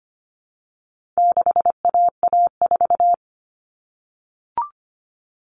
Morse code, heard as a keyed tone, about 25 wpm, 700 Hz. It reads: 6AA4
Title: All Cut Numbers - 4-Digit 25wpm